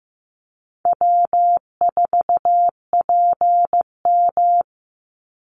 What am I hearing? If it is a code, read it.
Morse code, heard as a keyed tone, about 15 wpm, 700 Hz. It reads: W4PM
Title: Easy Call Signs Repeated 3x - 15wpm